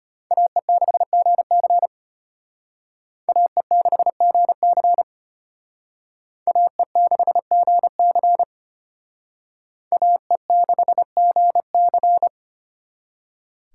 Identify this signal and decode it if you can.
Morse code, heard as a keyed tone, about 32 wpm, 700 Hz. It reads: AE6GC AE6GC AE6GC AE6GC